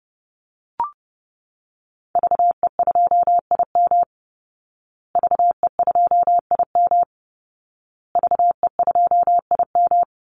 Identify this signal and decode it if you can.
Morse code, heard as a keyed tone, about 30 wpm, 700 Hz. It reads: VE2IM VE2IM VE2IM
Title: Call Signs Repeated 2x - 30wpm